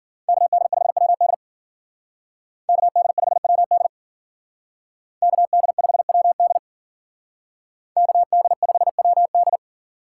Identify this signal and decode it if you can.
Morse code, heard as a keyed tone, about 48 wpm, 700 Hz. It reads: KD5WD KD5WD KD5WD KD5WD
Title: Speed Racing - USA 2x2 Call Signs 40wpm